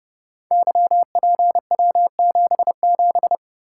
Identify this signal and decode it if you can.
Morse code, heard as a keyed tone, about 30 wpm, 700 Hz. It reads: YPW77
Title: Code Groups - 5 Letter:Numbers - 30wpm